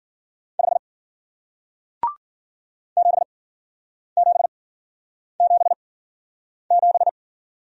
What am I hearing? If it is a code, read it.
Morse code, heard as a keyed tone, about 52 wpm, spaced as for 9 wpm, 700 Hz. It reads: 5 7777